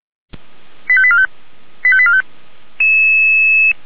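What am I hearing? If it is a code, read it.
Five-tone selective call (ZVEI), heard as quick runs of five tones, tones of about 70 ms. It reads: 85845 85845
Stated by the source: Five-tone ZVEI selective call received by SDR, from feuerwehr-orlamuende.org fire brigade audio archive.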